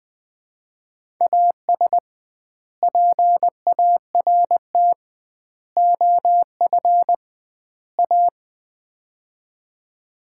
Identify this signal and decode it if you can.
Morse code, heard as a keyed tone, about 21 wpm, 700 Hz. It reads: AS PART OF A
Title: 2x Word-Spacing - Sets of 4 Words - 20wpm